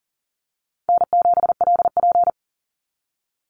Morse code, heard as a keyed tone, about 40 wpm, 700 Hz. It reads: N7LP
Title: NA SOTA Chaser Callsigns 40wpm